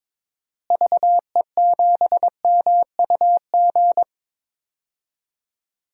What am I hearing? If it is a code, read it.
Morse code, heard as a keyed tone, about 22 wpm, 700 Hz. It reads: VE7MUG
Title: DXCC-100 Call Signs - 6 Characters - Repeated 3x - 22wpm